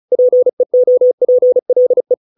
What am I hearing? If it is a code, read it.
Morse code, heard as a keyed tone, about 34 wpm, 500 Hz. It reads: PEOPLE